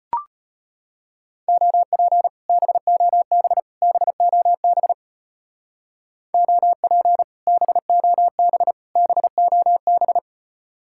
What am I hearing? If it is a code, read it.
Morse code, heard as a keyed tone, about 31 wpm, 700 Hz. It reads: OP BOB BOB OP BOB BOB